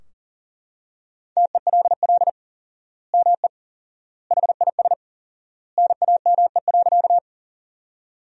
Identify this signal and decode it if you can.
Morse code, heard as a keyed tone, about 40 wpm, 700 Hz. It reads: TELL ME HIS NAME.
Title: 4x Word-Spacing - Sentences from Top 200 Words - 40wpm